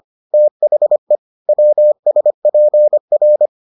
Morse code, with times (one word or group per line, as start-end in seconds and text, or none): 0.3-1.2 THE
1.5-3.5 WSPR